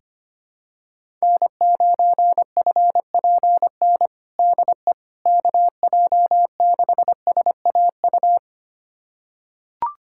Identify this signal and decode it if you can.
Morse code, heard as a keyed tone, about 25 wpm, 700 Hz. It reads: N9FPN DE KJ6SAU
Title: USA Callsign DE Callsign 25wpm